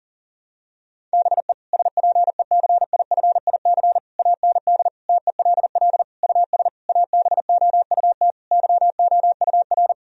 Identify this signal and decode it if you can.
Morse code, heard as a keyed tone, about 40 wpm, 700 Hz. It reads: BE SPECIFIC AND TELL US ABOUT YOUR
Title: Sentences from Top 1000 Words - Review Part 3 - 40wpm